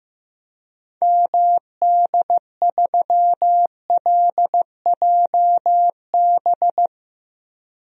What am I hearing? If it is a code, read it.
Morse code, heard as a keyed tone, about 16 wpm, 700 Hz. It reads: MD3LJB
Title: DXCC-100 Call Signs - 6 Characters - 15wpm